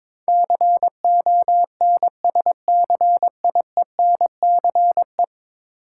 Morse code, heard as a keyed tone, about 21 wpm, 700 Hz. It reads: CONSCIENCE
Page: morse-code-ninja-practice-22wpm